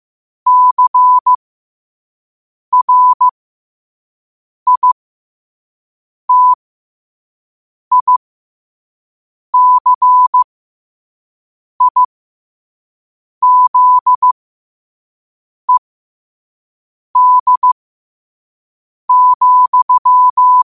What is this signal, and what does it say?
Morse code, heard as a keyed tone, about 15 wpm, spaced as for 5 wpm, 1000 Hz. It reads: CRITICIZED,
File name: qotdfiles / 5wpm